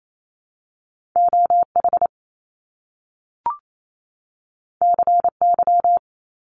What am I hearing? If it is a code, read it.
Morse code, heard as a keyed tone, about 28 wpm, 700 Hz. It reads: O H C Y
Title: Top Two Letter Combinations in Words 28wpm